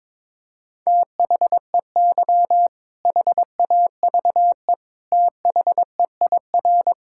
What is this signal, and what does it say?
Morse code, heard as a keyed tone, about 22 wpm, 700 Hz. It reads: THEY HAVE THEIR
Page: morse-code-ninja-practice-22wpm